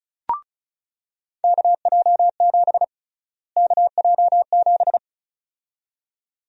Morse code, heard as a keyed tone, about 36 wpm, 700 Hz. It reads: KJ7 KJ7
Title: USA Call Sign Prefixes - Repeated 2x - 35wpm